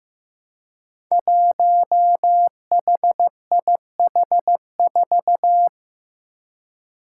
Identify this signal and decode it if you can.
Morse code, heard as a keyed tone, about 15 wpm, 700 Hz. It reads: 1HIH4
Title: Five Challenging Letter-Number Combinations 15wpm